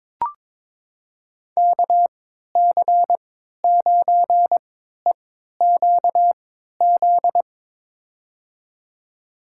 Morse code, heard as a keyed tone, about 22 wpm, spaced as for 12 wpm, 700 Hz. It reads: KC9EQZ
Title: USA Call Signs - 6 Characters - 3x Character Spacing - 22wpm